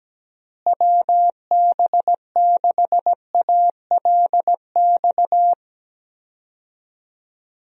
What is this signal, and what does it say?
Morse code, heard as a keyed tone, about 17 wpm, 700 Hz. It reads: WB6ALX
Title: USA Call Signs - 6 Characters - Repeated 3x - 17wpm